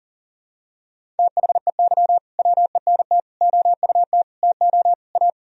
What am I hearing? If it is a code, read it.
Morse code, heard as a keyed tone, about 40 wpm, 700 Hz. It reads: THEY WENT OUT TO A
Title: Sentences from Top 1000 Words - Review Part 2 - 40wpm